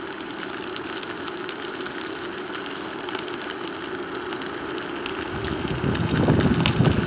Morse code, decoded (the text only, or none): none